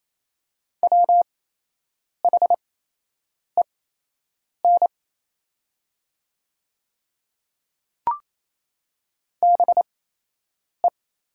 Morse code, heard as a keyed tone, about 28 wpm, spaced as for 8 wpm, 700 Hz. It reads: WHEN BE
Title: 8x Character Spacing - Top 300 Words - 28wpm